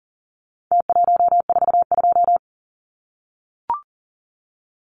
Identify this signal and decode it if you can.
Morse code, heard as a keyed tone, about 40 wpm, 700 Hz. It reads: T142
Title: Numbers - Contest Style - 4-Digit 40wpm